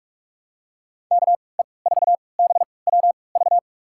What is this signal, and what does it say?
Morse code, heard as a keyed tone, about 46 wpm, spaced as for 25 wpm, 700 Hz. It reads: KE4BWV
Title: USA Call Signs - 6 Characters - 3x Character Spacing - 45wpm